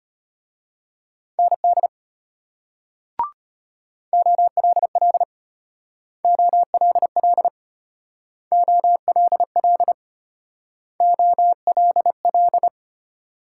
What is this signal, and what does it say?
Morse code, heard as a keyed tone, about 32 wpm, 700 Hz. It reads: ND OLL OLL OLL OLL